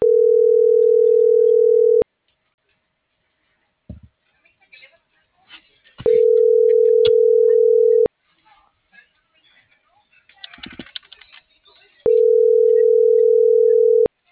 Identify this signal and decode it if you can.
Ringback tone (North America), 440 + 480 Hz.